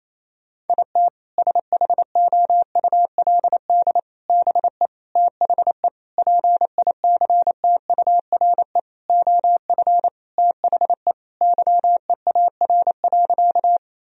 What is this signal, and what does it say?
Morse code, heard as a keyed tone, about 28 wpm, 700 Hz. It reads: IT SHOULD BE THE PICTURE OF THE YEAR.
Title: Sentences from Top 1000 Words - Review Part 1 - 28wpm